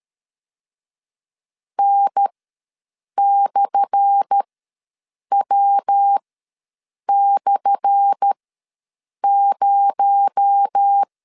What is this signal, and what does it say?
Morse code, heard as a keyed tone, about 13 wpm, spaced as for 7 wpm, 800 Hz. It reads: N/W/0
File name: LEZIONI / 10 SIGNS / 10A SLASH ws 12wpm 3443